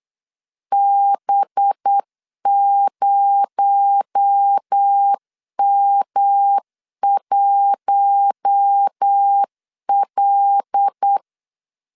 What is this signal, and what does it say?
Morse code, heard as a keyed tone, about 8 wpm, 800 Hz. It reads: B0M1L
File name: LEZIONI / 7 GF9Z / 7 G 8wpm 3118